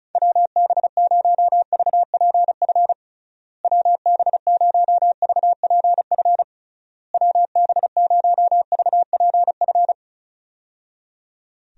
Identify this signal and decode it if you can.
Morse code, heard as a keyed tone, about 36 wpm, 700 Hz. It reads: WB0VPF WB0VPF WB0VPF
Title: USA Call Signs - 6 Characters - Repeated 3x - 35wpm